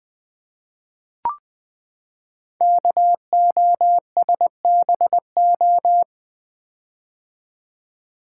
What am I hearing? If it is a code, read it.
Morse code, heard as a keyed tone, about 20 wpm, 700 Hz. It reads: KOSBO
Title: Code Groups - 5 Letters - 20wpm